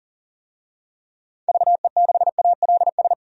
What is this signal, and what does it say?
Morse code, heard as a keyed tone, about 37 wpm, 700 Hz. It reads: VE6ALS
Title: DXCC-100 Call Signs - 6 Characters - 40wpm